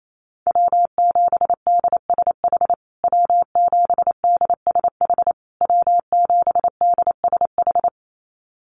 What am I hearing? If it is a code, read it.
Morse code, heard as a keyed tone, about 28 wpm, 690 Hz. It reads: W7DSH W7DSH W7DSH
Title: USA Call Signs Repeated 3x 28wpm